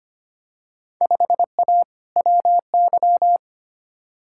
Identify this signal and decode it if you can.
Morse code, heard as a keyed tone, about 25 wpm, 700 Hz. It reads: 5A WY
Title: Field Day Exchanges Encore - 25wpm